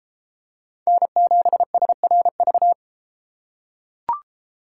Morse code, heard as a keyed tone, about 32 wpm, 700 Hz. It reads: N7SRV